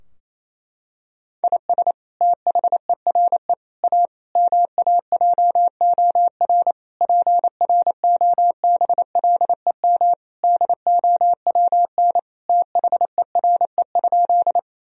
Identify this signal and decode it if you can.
Morse code, heard as a keyed tone, about 28 wpm, 700 Hz. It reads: IS THERE A MAJOR PROBLEM DOWN THERE?
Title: Sentences from Top 1000 Words - Review Part 4 - 28wpm